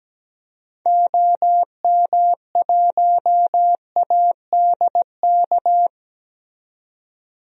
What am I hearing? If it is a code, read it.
Morse code, heard as a keyed tone, about 17 wpm, 700 Hz. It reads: OM1ADK
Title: DXCC-100 Call Signs - 6 Characters - 17wpm